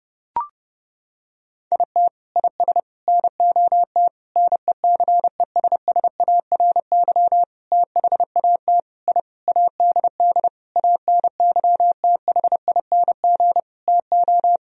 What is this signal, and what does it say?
Morse code, heard as a keyed tone, about 30 wpm, 700 Hz. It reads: IT IS NOT NECESSARY THAT I ADD ANYTHING TO
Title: Sentences from Top 1000 Words - Review Part 1 - 30wpm